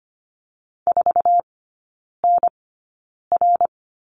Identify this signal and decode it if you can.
Morse code, heard as a keyed tone, about 25 wpm, spaced as for 9 wpm, 700 Hz. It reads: 4NR
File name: Lesson 17e - Letter u - Three Characters 25wpm